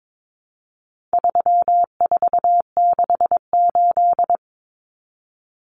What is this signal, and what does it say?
Morse code, heard as a keyed tone, about 22 wpm, 700 Hz. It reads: 3468
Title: Numbers - 4-Digit 22wpm